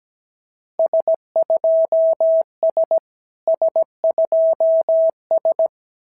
Morse code, heard as a keyed tone, about 16 wpm, 650 Hz. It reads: S2S S2S
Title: SOTA QSO Practice 17wpm